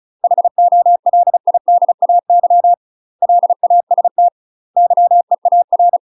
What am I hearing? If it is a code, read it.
Morse code, heard as a keyed tone, about 34 wpm, 700 Hz. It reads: HOLIDAY LAST YEAR
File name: UK MASTER QSOs TRAINING / 35WPM / UK MASTER QSOs 0025